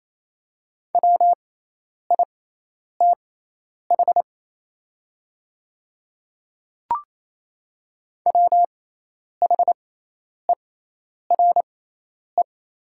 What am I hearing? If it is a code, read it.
Morse code, heard as a keyed tone, about 28 wpm, spaced as for 10 wpm, 700 Hz. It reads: WITH WHERE
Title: 6x Character Spacing - Top 300 Words - 28wpm